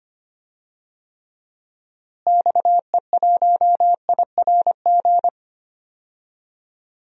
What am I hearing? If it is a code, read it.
Morse code, heard as a keyed tone, about 25 wpm, 700 Hz. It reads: XE1IRG
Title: DXCC-100 Call Signs - 6 Characters - Repeated 2x - 25wpm